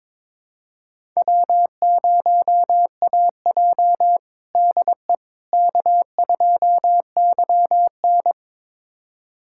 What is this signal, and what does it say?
Morse code, heard as a keyed tone, about 22 wpm, 700 Hz. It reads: W0AJ DE K2YN
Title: USA Callsign DE Callsign - 4 Characters - 22wpm